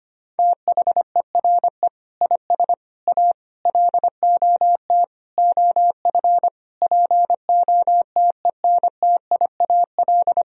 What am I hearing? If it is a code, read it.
Morse code, heard as a keyed tone, about 25 wpm, 700 Hz. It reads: THERE IS A LOT OF POTENTIAL
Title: Sentences from Top 1000 Words - Review Part 4 - 25wpm